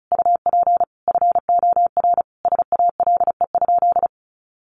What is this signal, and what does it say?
Morse code, heard as a keyed tone, about 34 wpm, 700 Hz. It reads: UP FOR SALE?
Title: Sentences from Top 1400 Words 35wpm